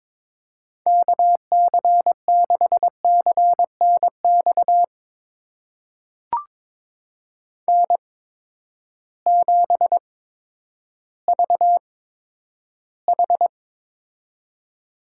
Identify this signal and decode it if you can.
Morse code, heard as a keyed tone, about 22 wpm, spaced as for 11 wpm, 700 Hz. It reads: KC6CNX N7VH
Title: USA Call Signs - 8x Character Spacing - 22wpm